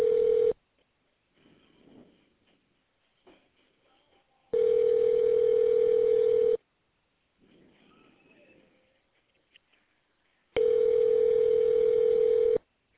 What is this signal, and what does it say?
Ringback tone (North America), 440 + 480 Hz.